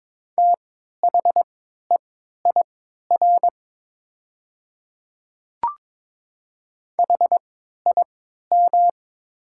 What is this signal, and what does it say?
Morse code, heard as a keyed tone, about 22 wpm, spaced as for 12 wpm, 700 Hz. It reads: THEIR HIM
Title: 3x Character Spacing - Top 500 Words - 22wpm